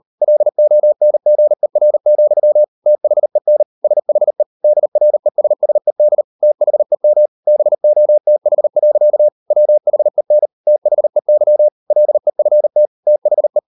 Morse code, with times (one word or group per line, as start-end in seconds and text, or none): 0.2-2.6 LONGER,
2.9-3.6 THEN
3.8-4.4 SHE
4.6-6.2 DRESSED
6.4-7.3 THEM
7.5-9.3 BOTH.
9.5-10.5 WHEN
10.7-11.7 THEY
11.9-12.9 LEFT
13.1-13.6 THE